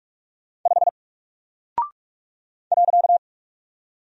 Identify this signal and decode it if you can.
Morse code, heard as a keyed tone, about 44 wpm, 700 Hz. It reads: 5 .
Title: Single Character (Letter:Number:Punctuation:Prosign) 45wpm